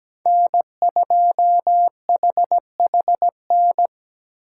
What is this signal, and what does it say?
Morse code, heard as a keyed tone, about 20 wpm, 700 Hz. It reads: N2HHN
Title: Speed Racing - USA 1x3 Call Signs 17wpm